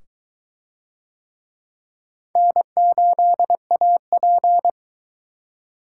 Morse code, heard as a keyed tone, about 22 wpm, 700 Hz. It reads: N8AP